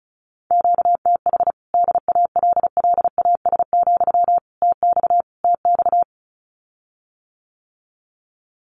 Morse code, heard as a keyed tone, about 34 wpm, 700 Hz. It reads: QTH DALLAS, TX TX
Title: Common QSO Phrases Encore 35wpm